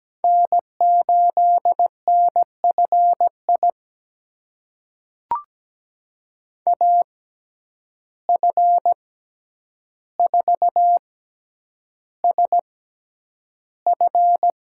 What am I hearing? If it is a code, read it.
Morse code, heard as a keyed tone, about 17 wpm, spaced as for 9 wpm, 700 Hz. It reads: N8NFI AF4SF
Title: USA Call Signs - 5 Characters - 6x Character Spacing - 17wpm